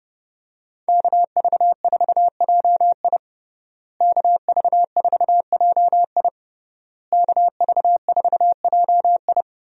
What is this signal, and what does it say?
Morse code, heard as a keyed tone, about 30 wpm, 700 Hz. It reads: KV4JI KV4JI KV4JI
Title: USA Call Signs - 5 Characters - Repeated 3x - 30wpm